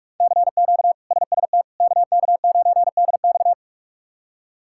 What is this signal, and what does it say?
Morse code, heard as a keyed tone, about 44 wpm, 700 Hz. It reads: CQ SST KK9DX
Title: K1USN SST QSO Practice - 45wpm